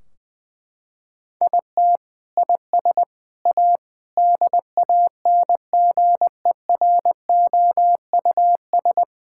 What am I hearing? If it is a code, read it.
Morse code, heard as a keyed tone, about 20 wpm, 700 Hz. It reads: IT IS A DANGEROUS
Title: Sentences from Top 1600 Words 20wpm